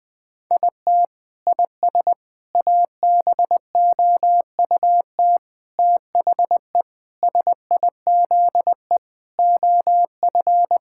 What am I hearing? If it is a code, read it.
Morse code, heard as a keyed tone, about 21 wpm, 700 Hz. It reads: IT IS ABOUT THE SIZE OF
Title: Sentences with 5-Word Phrases - Spoken Beginning - 20wpm